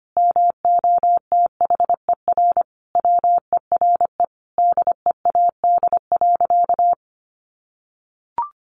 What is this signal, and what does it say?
Morse code, heard as a keyed tone, about 25 wpm, 700 Hz. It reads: MOTHER WERE DEAD.
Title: Sentences from Top 1500 Words 25wpm